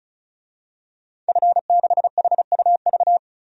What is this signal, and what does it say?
Morse code, heard as a keyed tone, about 34 wpm, 700 Hz. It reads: F6HUV
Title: DXCC-100 Call Signs - 5 Characters - 35wpm